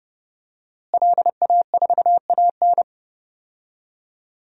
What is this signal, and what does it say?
Morse code, heard as a keyed tone, about 30 wpm, 700 Hz. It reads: LA4AN
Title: DXCC-100 Call Signs - 5 Characters - Repeated 2x - 30wpm